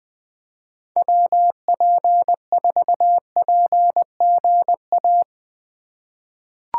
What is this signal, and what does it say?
Morse code, heard as a keyed tone, about 20 wpm, 700 Hz. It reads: WP4PGA
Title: USA Call Signs - 6 Characters - 2x Character Spacing - 20wpm